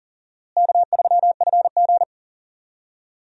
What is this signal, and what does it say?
Morse code, heard as a keyed tone, about 40 wpm, 700 Hz. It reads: K3FG
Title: USA Call Signs - 4 Characters - Repeated 2x - 40wpm